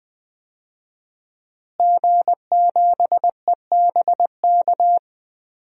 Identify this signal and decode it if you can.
Morse code, heard as a keyed tone, about 20 wpm, 700 Hz. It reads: G7EBK
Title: DXCC-100 Call Signs - 5 Characters - Repeated 2x - 20wpm